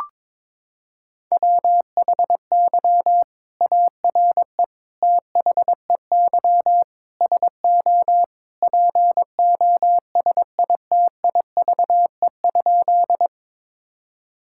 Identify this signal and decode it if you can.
Morse code, heard as a keyed tone, about 22 wpm, 700 Hz. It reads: WHY ARE THEY SO POSITIVE?
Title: Sentences from Top 1000 Words - Review Part 5 - 22wpm